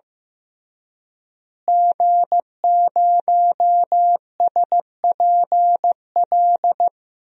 Morse code, heard as a keyed tone, about 16 wpm, 700 Hz. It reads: G0SPL
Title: DXCC-100 Call Signs - 5 Characters - Repeated 3x - 15wpm